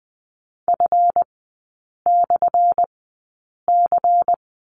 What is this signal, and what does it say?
Morse code, heard as a keyed tone, about 20 wpm, spaced as for 8 wpm, 700 Hz. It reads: F/C
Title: 178 Lesson 29e - Slash - Three Characters 20wpm